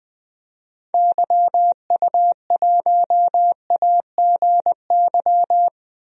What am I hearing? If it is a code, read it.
Morse code, heard as a keyed tone, about 20 wpm, 700 Hz. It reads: YU1AGY
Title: DXCC-100 Call Signs - 6 Characters - 20wpm